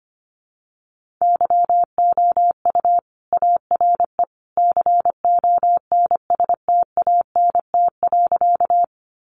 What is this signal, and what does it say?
Morse code, heard as a keyed tone, about 25 wpm, 700 Hz. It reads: YOU ARE CONSTANT.
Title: Sentences from Top 1500 Words 25wpm